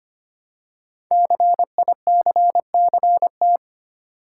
Morse code, heard as a keyed tone, about 25 wpm, 700 Hz. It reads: CICCT
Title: Code Groups - 5 Letters - 25wpm